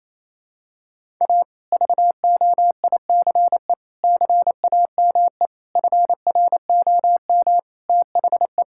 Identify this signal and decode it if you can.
Morse code, heard as a keyed tone, about 28 wpm, 700 Hz. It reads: A VOICE CAME FROM THE
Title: Sentences from Top 1000 Words - Review Part 5 - 28wpm